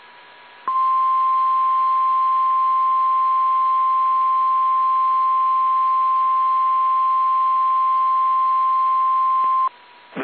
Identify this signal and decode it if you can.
Emergency Alert System (SAME) tones (the 1050 Hz attention tone). It is heard from 1.0 to 9.1 s.
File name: eas / KSDZKDJL / KSDZKDJL